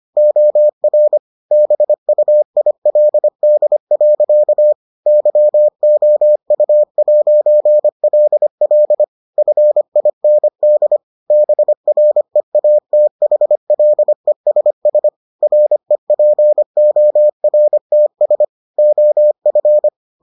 Morse code, heard as a keyed tone, about 25 wpm, 600 Hz. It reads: OR BUILD. YOU'LL FIND BREATHLESS REPORTS OF